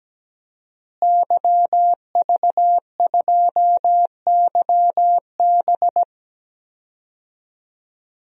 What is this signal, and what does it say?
Morse code, heard as a keyed tone, about 17 wpm, 700 Hz. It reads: YV2YB
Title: DXCC-100 Call Signs - 5 Characters - Repeated 3x - 17wpm